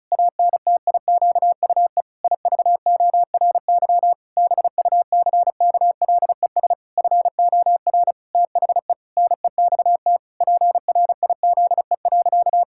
Morse code, heard as a keyed tone, about 36 wpm, 700 Hz. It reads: ANTIQUE IVORY BUCKLES FOR THE NEXT PRIZE.
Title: Pangram Challenge v2 - 35wpm